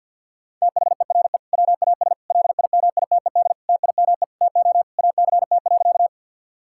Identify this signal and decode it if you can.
Morse code, heard as a keyed tone, about 52 wpm, 700 Hz. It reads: THERE WAS LIMITED TIME TO ACT.
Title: Sentences from Top 1000 Words - Review Part 6 - 50wpm